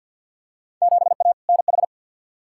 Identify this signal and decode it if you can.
Morse code, heard as a keyed tone, about 53 wpm, 700 Hz. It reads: 7A NH
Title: Field Day Exchanges - 50wpm